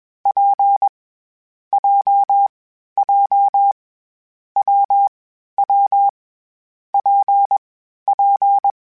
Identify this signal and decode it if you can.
Morse code, heard as a keyed tone, about 21 wpm, 800 Hz. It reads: P J J W W P P
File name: Variable Speed / Variable Speed-mp3s / WPJ WPJ1 sp @20